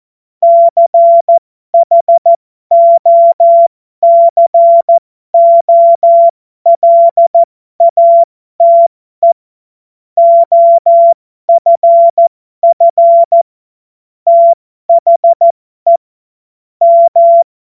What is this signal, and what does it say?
Morse code, heard as a keyed tone, about 14 wpm, spaced as for 12 wpm, 680 Hz. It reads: CHOCOLATE OFF THE M